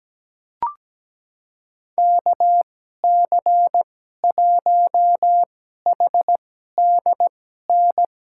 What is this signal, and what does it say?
Morse code, heard as a keyed tone, about 17 wpm, spaced as for 12 wpm, 700 Hz. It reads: KC1HDN
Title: USA Call Signs - 2x Character Spacing - 17wpm